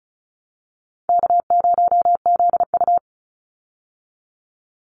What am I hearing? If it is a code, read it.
Morse code, heard as a keyed tone, about 34 wpm, 700 Hz. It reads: K0ZU
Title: Easy Call Signs Repeated 2x - 35wpm